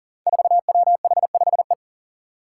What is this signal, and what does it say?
Morse code, heard as a keyed tone, about 39 wpm, 700 Hz. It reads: 4WH5E
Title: Five Challenging Letter-Number Combinations 40wpm